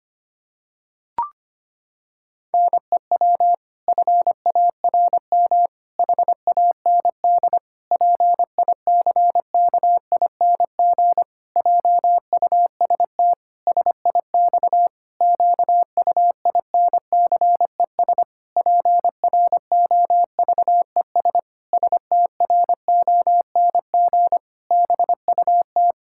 Morse code, heard as a keyed tone, about 25 wpm, 700 Hz. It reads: NEW FARM HAND PICKING JUST SIX QUINCES PROVES STRONG BUT
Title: Pangram Challenge v2 - 25wpm